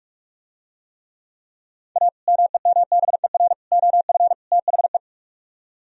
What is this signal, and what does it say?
Morse code, heard as a keyed tone, about 47 wpm, 700 Hz. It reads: A MEMBER OF THE